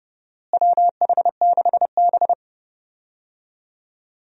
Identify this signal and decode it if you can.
Morse code, heard as a keyed tone, about 29 wpm, 700 Hz. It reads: WH6B
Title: USA Call Signs - 4 Characters - Repeated 2x - 30wpm